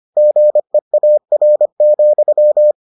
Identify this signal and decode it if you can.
Morse code, heard as a keyed tone, about 25 wpm, 600 Hz. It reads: GEAR,